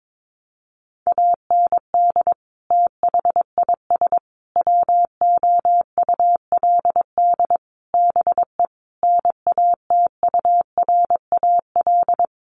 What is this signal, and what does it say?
Morse code, heard as a keyed tone, about 22 wpm, 700 Hz. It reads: AND THIS WOULD BE NATURAL
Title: Sentences from Top 1400 Words 22wpm